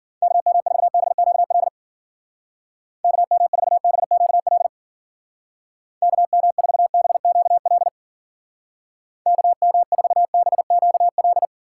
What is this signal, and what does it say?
Morse code, heard as a keyed tone, about 49 wpm, 700 Hz. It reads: KM4BQL KM4BQL KM4BQL KM4BQL
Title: Speed Racing - USA 2x3 Call Signs 40wpm